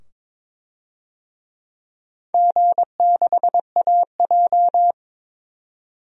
Morse code, heard as a keyed tone, about 22 wpm, 700 Hz. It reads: G6AJ
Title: DXCC-100 Call Signs - 4 Characters - 22wpm